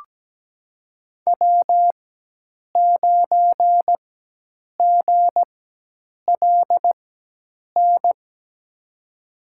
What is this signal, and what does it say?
Morse code, heard as a keyed tone, about 17 wpm, spaced as for 8 wpm, 700 Hz. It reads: W9GLN
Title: USA Call Signs - 5 Characters - 4x Character Spacing - 17wpm